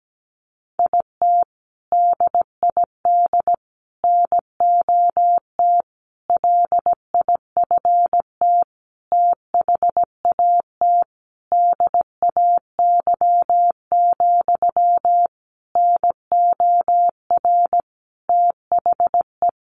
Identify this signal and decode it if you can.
Morse code, heard as a keyed tone, about 17 wpm, 700 Hz. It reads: IT DID NOT LIFT THAT DAY, NOR THE
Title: Sentences from Top 1500 Words 17wpm